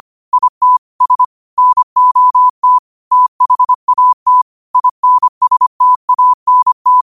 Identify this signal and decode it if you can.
Morse code, heard as a keyed tone, about 25 wpm, 1000 Hz. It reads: IT S NOT THAT INSTANT
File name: qotdfiles / 25wpm